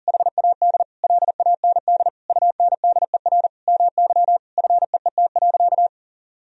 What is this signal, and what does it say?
Morse code, heard as a keyed tone, about 40 wpm, 700 Hz. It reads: HAD LAND UNDER MY FEET.
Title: Sentences from Top 1000 Words - Review Part 1 - 40wpm